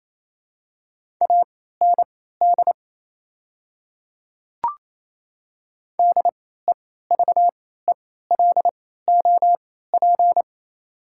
Morse code, heard as a keyed tone, about 28 wpm, spaced as for 16 wpm, 690 Hz. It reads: AND DEVELOP
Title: 3x Character Spacing - Top 500 Words - 28wpm